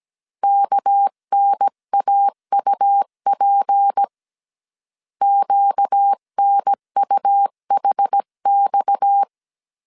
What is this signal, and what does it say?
Morse code, heard as a keyed tone, about 17 wpm, 800 Hz. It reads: KNAUP QNUHX
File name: CW / DET / 1 6 1500C S C 16wpm 2733